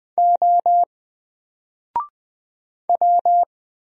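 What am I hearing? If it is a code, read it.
Morse code, heard as a keyed tone, about 20 wpm, 700 Hz. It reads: O W
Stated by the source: Single letter and number Morse drill at 20 wpm, from N4LCD.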